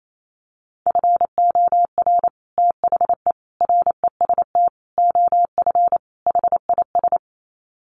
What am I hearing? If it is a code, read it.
Morse code, heard as a keyed tone, about 28 wpm, 700 Hz. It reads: FOR THE REST OF HIS
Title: Sets of 5 Words 28wpm